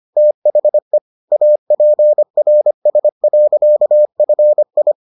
Morse code, heard as a keyed tone, about 25 wpm, 600 Hz. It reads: THE APRS.FI